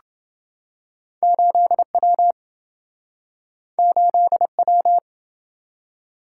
Morse code, heard as a keyed tone, about 27 wpm, 700 Hz. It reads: 8W 8W